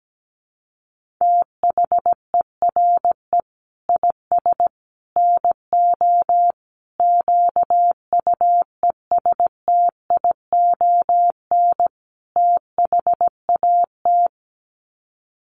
Morse code, heard as a keyed tone, about 17 wpm, 700 Hz. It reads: THERE IS NO QUESTION THAT
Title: Sets of 5 Words Encore 17wpm